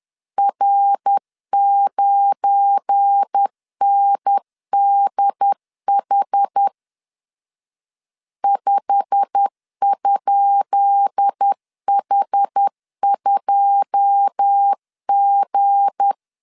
Morse code, heard as a keyed tone, about 10 wpm, 800 Hz. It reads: R9NDH 5?H2G
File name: LEZIONI / 10 SIGNS / 10D COMA 10wpm 3013